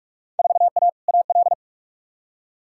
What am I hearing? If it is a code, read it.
Morse code, heard as a keyed tone, about 47 wpm, 700 Hz. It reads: 4A AL
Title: Field Day Exchanges - 45wpm